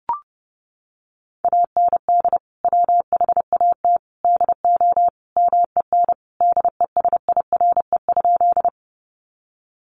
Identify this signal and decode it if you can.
Morse code, heard as a keyed tone, about 30 wpm, 700 Hz. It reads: AND WHAT DO MEN DESIRE?
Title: Sentences from Top 1100 Words 30wpm